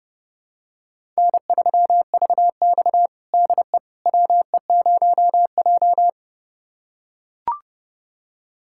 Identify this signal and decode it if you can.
Morse code, heard as a keyed tone, about 30 wpm, 700 Hz. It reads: N3VX DE WE0J
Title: USA Callsign DE Callsign - 4 Characters - 30wpm